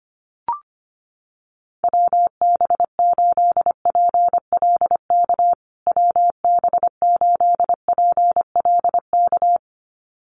Morse code, heard as a keyed tone, about 25 wpm, 700 Hz. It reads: WB8PLK WB8PLK
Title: USA Call Signs Repeated 2x 25wpm